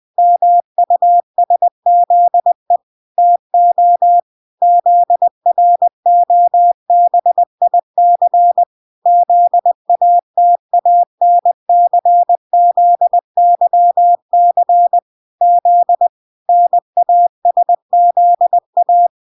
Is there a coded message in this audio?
Yes: Morse code from 0.2 to 19.1 s, heard as a keyed tone, about 20 wpm, 700 Hz. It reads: MUSZE TO ZROBIC ZATANCZYC Z NASZA